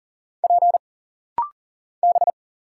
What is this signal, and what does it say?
Morse code, heard as a keyed tone, about 39 wpm, 700 Hz. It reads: P B
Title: Single Letters - Rapid-Fire 40wpm